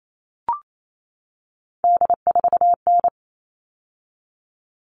Morse code, heard as a keyed tone, about 28 wpm, 700 Hz. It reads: D4N
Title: All Cut Numbers - 3-Digit 28wpm